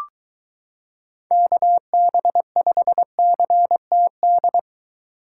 Morse code, heard as a keyed tone, about 22 wpm, 700 Hz. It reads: KB5CTD